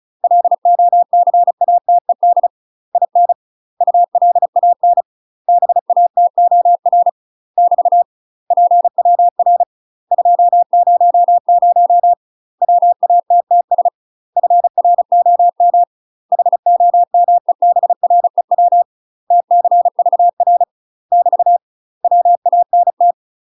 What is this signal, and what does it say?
Morse code, heard as a keyed tone, about 34 wpm, 700 Hz. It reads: LOCATED IN ULAN BATOR <BT> PWR 200 WATTS FROM HOMEBREW TCVR <BT> WANT
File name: UK MASTER QSOs TRAINING/35WPM